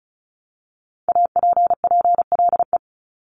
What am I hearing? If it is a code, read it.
Morse code, heard as a keyed tone, about 36 wpm, 700 Hz. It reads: APPLE
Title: Multi-letter Phonemes 35wpm